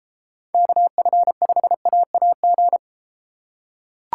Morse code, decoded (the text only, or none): KF5AAG